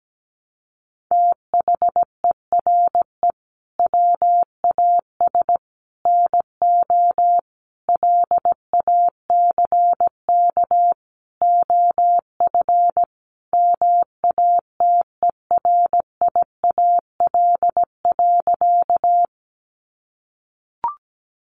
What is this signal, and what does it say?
Morse code, heard as a keyed tone, about 17 wpm, 700 Hz. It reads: THERE WAS NO LACK OF MATERIAL.
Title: Sentences from Top 1100 Words 17wpm